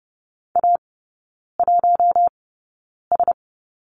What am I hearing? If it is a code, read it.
Morse code, heard as a keyed tone, about 30 wpm, spaced as for 9 wpm, 700 Hz. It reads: A1S
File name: 081 Lesson 14e - Number 2 - Three Characters 30wpm